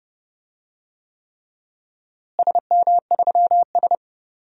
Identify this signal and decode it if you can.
Morse code, heard as a keyed tone, about 30 wpm, 700 Hz. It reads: SM3S
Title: DXCC-100 Call Signs - 4 Characters - Repeated 3x - 30wpm